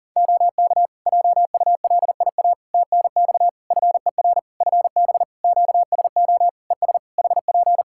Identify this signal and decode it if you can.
Morse code, heard as a keyed tone, about 40 wpm, 700 Hz. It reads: OK JULIA TNX FER FB QSO ES HP
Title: Practice QSO - Exchange 4 - Ending - 40wpm